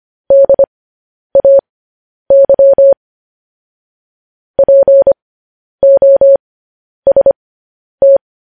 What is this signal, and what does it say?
Morse code, heard as a keyed tone, about 25 wpm, spaced as for 10 wpm, 550 Hz. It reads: DAY POST
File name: podcast / News headlines-1x-25x10